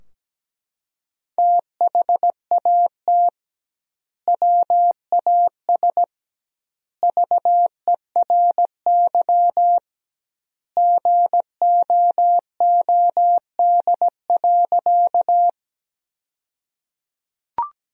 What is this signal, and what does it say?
Morse code, heard as a keyed tone, about 17 wpm, 700 Hz. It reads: THAT WAS VERY GOOD.
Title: 2x Word-Spacing - Sentences from Top 200 Words - 17wpm